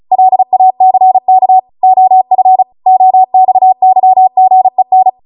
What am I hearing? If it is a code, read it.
Morse code, heard as a keyed tone, about 36 wpm, 750 Hz. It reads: LACK OF OXYGEN
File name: Morse / 35WPM